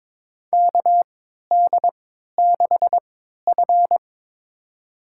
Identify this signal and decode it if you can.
Morse code, heard as a keyed tone, about 22 wpm, spaced as for 13 wpm, 700 Hz. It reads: KD6F
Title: USA Call Signs - 4 Characters - 3x Character Spacing - 22wpm